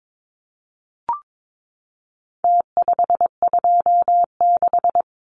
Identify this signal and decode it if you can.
Morse code, heard as a keyed tone, about 22 wpm, 700 Hz. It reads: T526
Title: Numbers - Contest Style - 4-Digit 22wpm